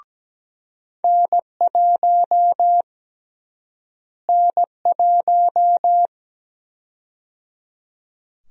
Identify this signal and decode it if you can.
Morse code, heard as a keyed tone, about 17 wpm, 700 Hz. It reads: N1 N1
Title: USA Call Sign Prefixes - Repeated 2x - 17wpm